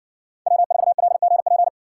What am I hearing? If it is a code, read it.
Morse code, heard as a keyed tone, about 60 wpm, 700 Hz. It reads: W4LGP